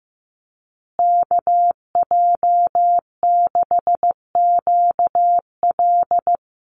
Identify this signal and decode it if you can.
Morse code, heard as a keyed tone, about 15 wpm, 700 Hz. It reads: KJ6QL
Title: USA Call Signs Repeated 3x 15wpm